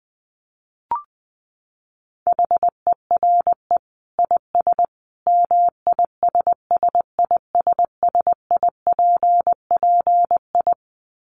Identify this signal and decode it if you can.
Morse code, heard as a keyed tone, about 20 wpm, 700 Hz. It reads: HERE IS MISSISSIPPI
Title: Dit Challenge 20wpm